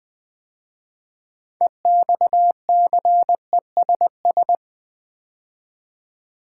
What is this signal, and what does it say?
Morse code, heard as a keyed tone, about 20 wpm, 700 Hz. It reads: EXCESS
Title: 6 Letter Words - First 3 Letters Given - 20wpm